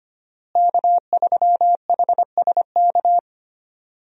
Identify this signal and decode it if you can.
Morse code, heard as a keyed tone, about 25 wpm, 700 Hz. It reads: K3HSK
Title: USA Call Signs - 5 Characters - Repeated 2x - 25wpm